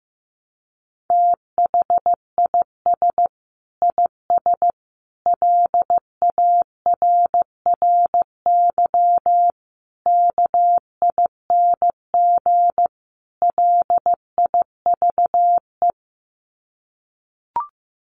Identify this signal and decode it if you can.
Morse code, heard as a keyed tone, about 15 wpm, 700 Hz. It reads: THIS IS LARRY KING LIVE
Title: Sets of 5 Words Encore2 15wpm